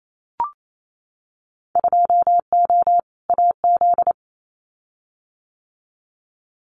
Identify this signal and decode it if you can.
Morse code, heard as a keyed tone, about 28 wpm, 700 Hz. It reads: 2O AZ
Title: Winter Field Day Exchanges 28wpm